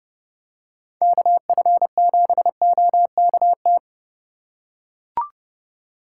Morse code, heard as a keyed tone, about 30 wpm, 700 Hz. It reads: KF7OKT